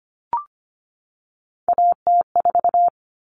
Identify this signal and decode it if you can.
Morse code, heard as a keyed tone, about 25 wpm, 700 Hz. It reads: AT4
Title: Common Cut Numbers - 3-Digit 25wpm